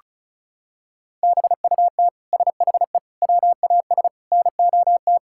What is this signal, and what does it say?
Morse code, heard as a keyed tone, about 36 wpm, 700 Hz. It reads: BUT SHE WAS NOT
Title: Sentences from Top 1000 Words - Review Part 5 - 35wpm